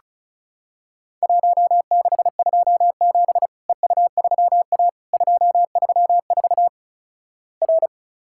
Morse code, heard as a keyed tone, about 36 wpm, 700 Hz. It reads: 1627 EU3A 234 R
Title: Worked All Europe DX Contest - QTC Exchange - 35wpm